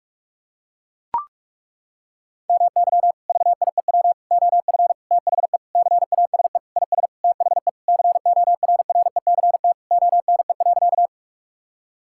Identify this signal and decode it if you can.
Morse code, heard as a keyed tone, about 42 wpm, 700 Hz. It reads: MY VIEW OF THE CASE IS THE CORRECT ONE.
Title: Sentences from Top 1000 Words - Review Part 6 - 45wpm